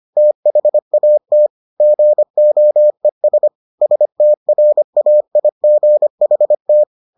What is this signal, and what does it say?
Morse code, heard as a keyed tone, about 25 wpm, 600 Hz. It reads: THAT GOES STRAIGHT